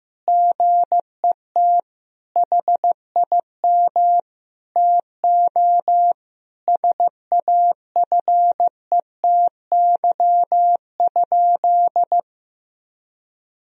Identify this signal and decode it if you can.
Morse code, heard as a keyed tone, about 16 wpm, 700 Hz. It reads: GET HIM TO SAFETY?
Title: Sentences from Top 1000 Words - Review Part 4 - 15wpm